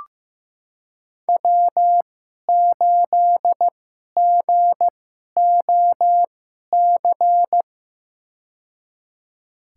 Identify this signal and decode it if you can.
Morse code, heard as a keyed tone, about 15 wpm, spaced as for 11 wpm, 700 Hz. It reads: W8GOC
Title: USA Call Signs - 5 Characters - 2x Character Spacing - 15wpm